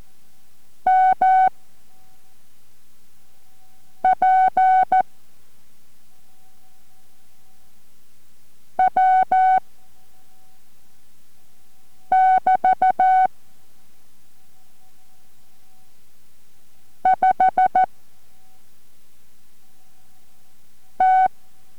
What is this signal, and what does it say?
Morse code, heard as a keyed tone, about 14 wpm, 750 Hz. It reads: M P W <BT> 5 T